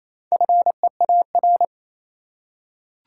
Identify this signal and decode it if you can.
Morse code, heard as a keyed tone, about 28 wpm, 700 Hz. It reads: FEAR
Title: Four Letter Words - 28wpm